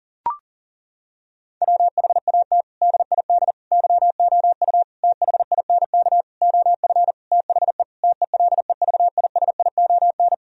Morse code, heard as a keyed tone, about 40 wpm, 700 Hz. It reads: WHAT DID YOU THINK OF THE TELEVISION
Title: Sentences from Top 1000 Words - Review Part 4 - 40wpm